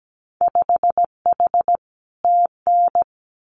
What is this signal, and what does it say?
Morse code, heard as a keyed tone, about 17 wpm, 700 Hz. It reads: 5H TN
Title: Winter Field Day Exchanges Encore 17wpm